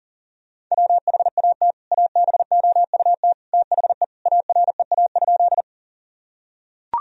Morse code, heard as a keyed tone, about 40 wpm, 700 Hz. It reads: WHAT ABOUT THE AREA?
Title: Sentences from Top 1000 Words - Review Part 2 - 40wpm